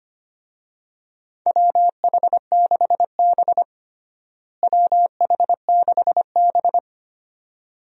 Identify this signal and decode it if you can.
Morse code, heard as a keyed tone, about 25 wpm, 700 Hz. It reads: WH6B WH6B
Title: USA Call Signs - 4 Characters - Repeated 2x - 25wpm